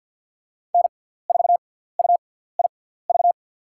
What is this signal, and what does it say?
Morse code, heard as a keyed tone, about 46 wpm, spaced as for 17 wpm, 700 Hz. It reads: N4UIV
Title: USA Call Signs - 6x Character Spacing - 50wpm